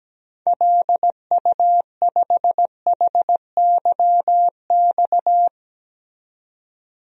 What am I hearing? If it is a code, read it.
Morse code, heard as a keyed tone, about 17 wpm, 700 Hz. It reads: LU5HYX
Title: DXCC-100 Call Signs - 6 Characters - 17wpm